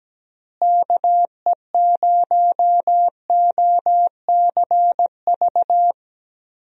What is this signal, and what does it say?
Morse code, heard as a keyed tone, about 17 wpm, 700 Hz. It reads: KE0OCV
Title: USA Call Signs - 6 Characters - Repeated 2x - 17wpm